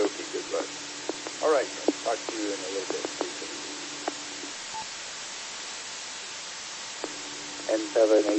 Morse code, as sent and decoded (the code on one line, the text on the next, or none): none